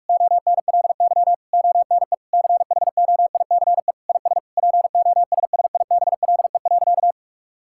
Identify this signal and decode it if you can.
Morse code, heard as a keyed tone, about 41 wpm, 700 Hz. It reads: ONLY ONE CHOICE IS POSSIBLE.
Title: Sentences from Top 1000 Words - Review Part 6 - 45wpm